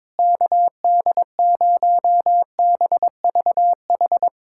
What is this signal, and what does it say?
Morse code, heard as a keyed tone, about 22 wpm, 700 Hz. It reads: KD0BVH
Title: USA Callsign DE Callsign - 6 Characters - 22wpm